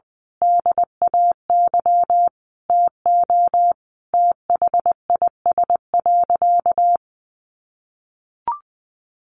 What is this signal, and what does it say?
Morse code, heard as a keyed tone, about 20 wpm, 700 Hz. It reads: DAY TO THIS.
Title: Sentences from Top 1400 Words 20wpm